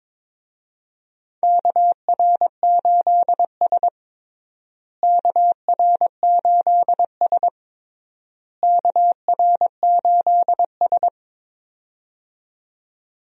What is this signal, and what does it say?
Morse code, heard as a keyed tone, about 22 wpm, 700 Hz. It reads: KR8S KR8S KR8S
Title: USA Call Signs - 4 Characters - Repeated 3x - 22wpm